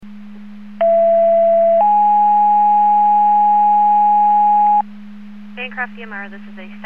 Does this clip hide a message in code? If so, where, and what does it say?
Yes: two-tone page from 0.8 to 4.8 s, heard as a one-second tone, then a three-second tone, about 670 then 830 Hz, with speech.